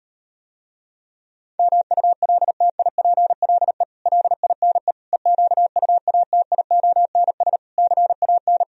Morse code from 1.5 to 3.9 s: MULTIPLE

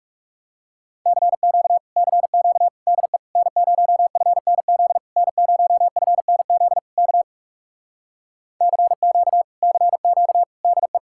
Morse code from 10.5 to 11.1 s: DE

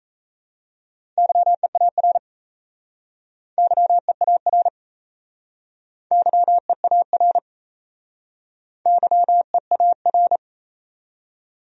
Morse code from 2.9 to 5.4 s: YEAR

